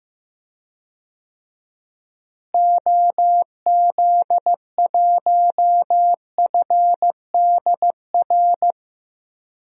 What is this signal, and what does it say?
Morse code, heard as a keyed tone, about 15 wpm, 700 Hz. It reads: OZ1FDR
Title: DXCC-100 Call Signs - 6 Characters - Repeated 2x - 15wpm